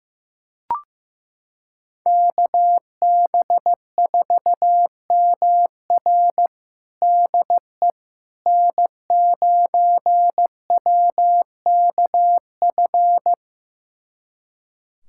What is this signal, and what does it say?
Morse code, heard as a keyed tone, about 15 wpm, 700 Hz. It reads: KB4MR DE N9WKF
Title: USA Callsign DE Callsign - 5 Characters - 15wpm